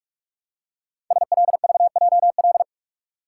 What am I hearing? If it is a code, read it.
Morse code, heard as a keyed tone, about 44 wpm, 700 Hz. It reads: SLVJL
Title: Code Groups - 5 Letters - 45wpm